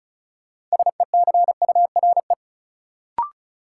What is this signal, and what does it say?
Morse code, heard as a keyed tone, about 36 wpm, 700 Hz. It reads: SECURE
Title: Sentences from Top 1200 Words - New Word Drill Alt - 35wpm